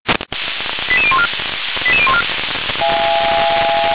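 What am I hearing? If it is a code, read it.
Five-tone selective call (ZVEI), heard as quick runs of five tones, tones of about 70 ms. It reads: 88014 88014
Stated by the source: ZVEI five-tone call received with an SDR.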